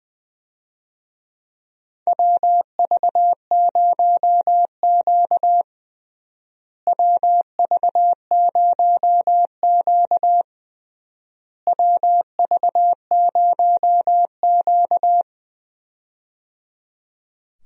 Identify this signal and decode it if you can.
Morse code, heard as a keyed tone, about 21 wpm, 700 Hz. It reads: WV0Q WV0Q WV0Q
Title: USA Call Signs - 4 Characters - Repeated 3x - 20wpm